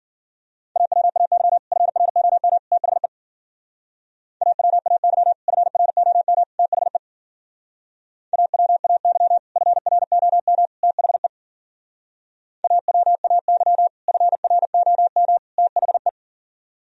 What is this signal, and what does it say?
Morse code, heard as a keyed tone, about 50 wpm, 700 Hz. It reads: AWAY FROM THE AWAY FROM THE AWAY FROM THE AWAY FROM THE
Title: Speed-Racing - Sets of 3 Words 40wpm